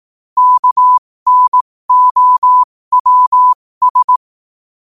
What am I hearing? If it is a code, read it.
Morse code, heard as a keyed tone, about 18 wpm, spaced as for 15 wpm, 1000 Hz. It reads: KNOWS
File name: qotdfiles / 15wpm